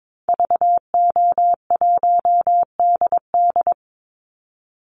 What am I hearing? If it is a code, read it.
Morse code, heard as a keyed tone, about 22 wpm, 700 Hz. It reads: VO1DD
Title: Call Signs Repeated 3x - 22wpm